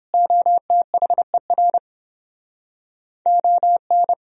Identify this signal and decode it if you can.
Morse code, heard as a keyed tone, about 29 wpm, 700 Hz. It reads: OTHER ON